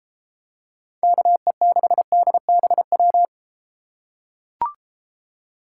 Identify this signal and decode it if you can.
Morse code, heard as a keyed tone, about 32 wpm, 700 Hz. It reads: KE6DBW